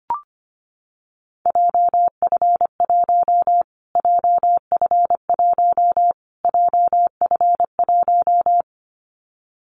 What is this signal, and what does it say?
Morse code, heard as a keyed tone, about 25 wpm, 700 Hz. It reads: JF1 JF1 JF1
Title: Callsign Prefixes for Top 25 DXCC - Repeated 3x - 25wpm